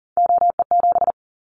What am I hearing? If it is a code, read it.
Morse code, heard as a keyed tone, about 37 wpm, 700 Hz. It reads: OE7
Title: Callsign Prefixes for Top 50 DXCC - 40wpm